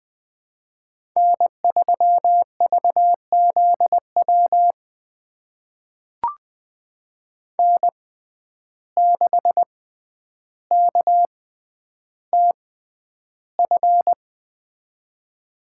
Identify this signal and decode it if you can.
Morse code, heard as a keyed tone, about 20 wpm, spaced as for 10 wpm, 700 Hz. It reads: N3VZW N6KTF
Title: USA Call Signs - 5 Characters - 6x Character Spacing - 20wpm